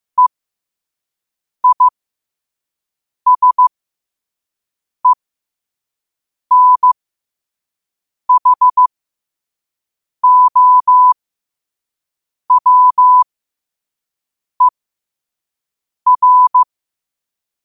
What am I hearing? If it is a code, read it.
Morse code, heard as a keyed tone, about 15 wpm, spaced as for 5 wpm, 1000 Hz. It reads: EISENHOWER
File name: qotdfiles / 5wpm